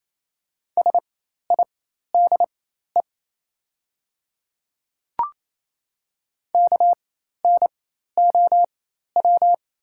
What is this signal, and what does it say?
Morse code, heard as a keyed tone, about 27 wpm, spaced as for 13 wpm, 700 Hz. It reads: SIDE KNOW
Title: 4x Character Spacing - Top 100 Words - 28wpm